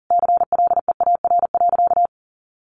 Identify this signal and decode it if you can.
Morse code, heard as a keyed tone, about 40 wpm, 700 Hz. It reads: CLEAR.
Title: Sentences from Top 1100 Words 40wpm